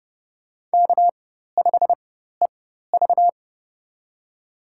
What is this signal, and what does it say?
Morse code, heard as a keyed tone, about 29 wpm, spaced as for 14 wpm, 700 Hz. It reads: K5EV
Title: USA Call Signs - 4 Characters - 4x Character Spacing - 30wpm